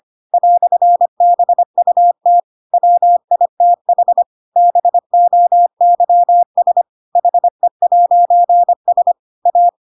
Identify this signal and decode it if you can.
Morse code, heard as a keyed tone, about 25 wpm, 700 Hz. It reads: "BUT WITH BOYS HE'S A
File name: Pinocchio / Pinocchio ch3 25 25 ik1hge